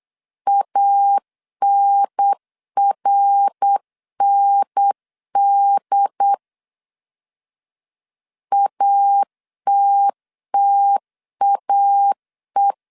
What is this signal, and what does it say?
Morse code, heard as a keyed tone, about 8 wpm, 800 Hz. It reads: ANRND ATTAE
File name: CW / DET / 1 3 1500C S C 8wpm 4214